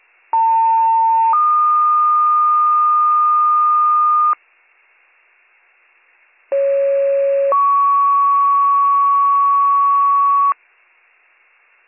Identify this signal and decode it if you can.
Two-tone page, heard as a one-second tone, then a three-second tone, about 900 then 1220 Hz, about 550 then 1120 Hz. It is heard from 0.3 to 10.5 s.